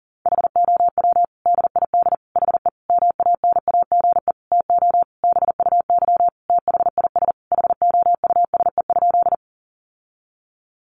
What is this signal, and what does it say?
Morse code, heard as a keyed tone, about 40 wpm, 700 Hz. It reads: HOW DID HE MANAGE TO BUY THIS HOUSE?
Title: Sentences from Top 1400 Words 40wpm